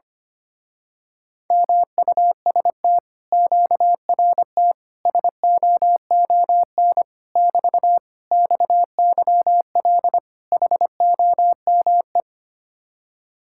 Morse code, heard as a keyed tone, about 24 wpm, 700 Hz. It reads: MUST QRT SOON <BT> XYL HOME
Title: Speed-Racing Common QSO Phrases - 25wpm